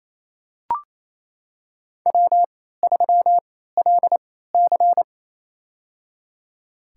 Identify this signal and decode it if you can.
Morse code, heard as a keyed tone, about 28 wpm, spaced as for 16 wpm, 700 Hz. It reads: W3LC
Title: USA Call Signs - 4 Characters - 3x Character Spacing - 28wpm